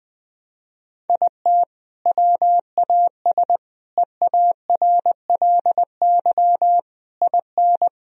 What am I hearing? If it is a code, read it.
Morse code, heard as a keyed tone, about 21 wpm, 700 Hz. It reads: IT WAS EARLY IN
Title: Sentences from Top 1000 Words - Review Part 2 - 20wpm